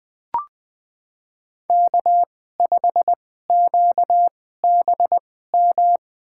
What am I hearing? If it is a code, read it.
Morse code, heard as a keyed tone, about 20 wpm, spaced as for 14 wpm, 700 Hz. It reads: K5QBM
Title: USA Call Signs - 5 Characters - 2x Character Spacing - 20wpm